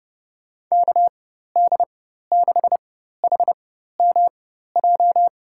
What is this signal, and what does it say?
Morse code, heard as a keyed tone, about 30 wpm, spaced as for 14 wpm, 700 Hz. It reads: KD6HMJ
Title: USA Call Signs - 6 Characters - 4x Character Spacing - 30wpm